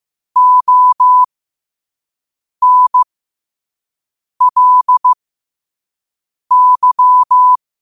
Morse code, heard as a keyed tone, about 15 wpm, spaced as for 5 wpm, 1000 Hz. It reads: ONLY
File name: qotdfiles / 5wpm